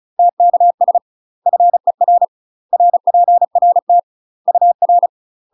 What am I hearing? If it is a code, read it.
Morse code, heard as a keyed tone, about 34 wpm, 700 Hz. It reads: TKS FER RPRT UR
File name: UK MASTER QSOs TRAINING / 35WPM / UK MASTER QSOs 0027